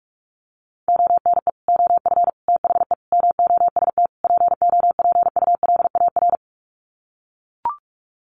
Morse code, heard as a keyed tone, about 44 wpm, 700 Hz. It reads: ONE OF THE MOST POPULAR
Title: Sets of 5 Words 45wpm